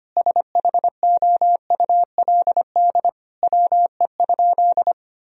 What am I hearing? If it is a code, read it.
Morse code, heard as a keyed tone, about 25 wpm, 700 Hz. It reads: SHOULD WE?
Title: Sentences from Top 1000 Words - Review Part 3 - 25wpm